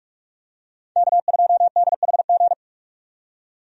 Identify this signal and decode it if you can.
Morse code, heard as a keyed tone, about 46 wpm, 700 Hz. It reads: K2DHG
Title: USA Call Signs - 5 Characters - 45wpm